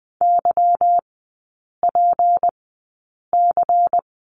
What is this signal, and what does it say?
Morse code, heard as a keyed tone, about 20 wpm, spaced as for 8 wpm, 700 Hz. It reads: YPC
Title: 152 Lesson 25e - Letter p - Three Characters 20wpm